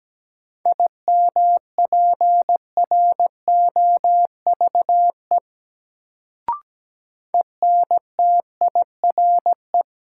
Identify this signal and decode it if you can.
Morse code, heard as a keyed tone, about 17 wpm, 700 Hz. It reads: IMPROVE ENTIRE
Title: Sentences from Top 800 Words - New Word Drill - 17wpm